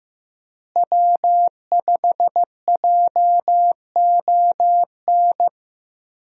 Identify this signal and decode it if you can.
Morse code, heard as a keyed tone, about 15 wpm, 700 Hz. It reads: W5JON
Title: USA Call Signs - 5 Characters - Repeated 2x - 15wpm